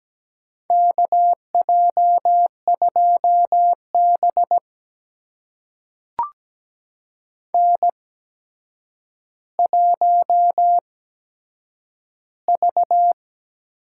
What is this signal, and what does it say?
Morse code, heard as a keyed tone, about 17 wpm, spaced as for 8 wpm, 700 Hz. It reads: KJ2B N1V
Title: USA Call Signs - 4 Characters - 8x Character Spacing - 17wpm